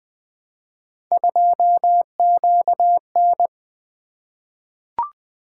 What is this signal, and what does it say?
Morse code, heard as a keyed tone, about 20 wpm, 700 Hz. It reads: 2QN
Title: Three Letter-Number Combinations - 20wpm